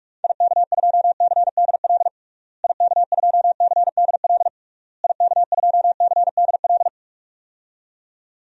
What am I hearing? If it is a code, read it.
Morse code, heard as a keyed tone, about 46 wpm, 700 Hz. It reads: IK2CDL IK2CDL IK2CDL
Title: DXCC-100 Call Signs - 6 Characters - Repeated 3x - 45wpm